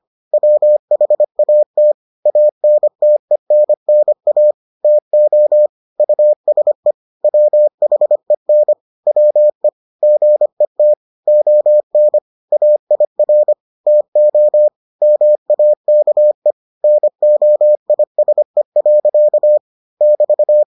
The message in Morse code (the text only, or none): WHAT ANTENNA TO USE WHEN WE GET ON AIR TO MAKE NOISE. <BT>